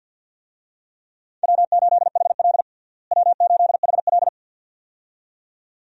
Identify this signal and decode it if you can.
Morse code, heard as a keyed tone, about 49 wpm, 700 Hz. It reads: W8HL W8HL
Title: USA Call Signs - 4 Characters - Repeated 2x - 50wpm